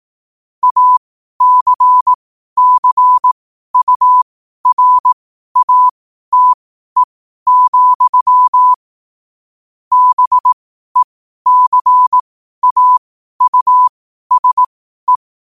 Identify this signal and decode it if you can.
Morse code, heard as a keyed tone, about 18 wpm, spaced as for 12 wpm, 1000 Hz. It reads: ACCURATE, BECAUSE
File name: qotdfiles / 12wpm